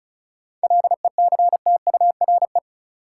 Morse code, heard as a keyed tone, about 36 wpm, 700 Hz. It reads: LECTURE
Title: Sentences from Top 1600 Words - New Word Drill Alt - 35wpm